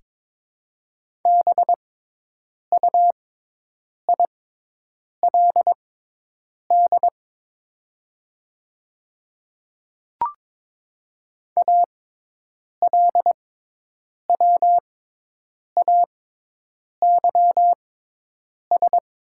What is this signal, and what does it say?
Morse code, heard as a keyed tone, about 21 wpm, spaced as for 8 wpm, 700 Hz. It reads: BUILD ALWAYS
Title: 6x Character Spacing - Top 500 Words - 22wpm